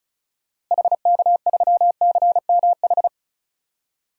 Morse code, heard as a keyed tone, about 36 wpm, 700 Hz. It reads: HK3CMH
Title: DXCC-100 Call Signs - 6 Characters - Repeated 3x - 35wpm